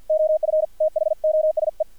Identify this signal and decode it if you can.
Morse code, heard as a keyed tone, about 50 wpm, 620 Hz, with radio noise. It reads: OF THOSE
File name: archived / CRSnet